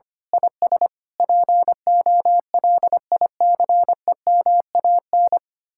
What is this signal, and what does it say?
Morse code, heard as a keyed tone, about 24 wpm, 700 Hz. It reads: IS POLICEMAN
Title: Speed-Racing Common QSO Phrases Encore - 25wpm